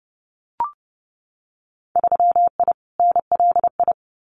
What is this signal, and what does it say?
Morse code, heard as a keyed tone, about 30 wpm, 700 Hz. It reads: 3I NLI
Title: Winter Field Day Exchanges 30wpm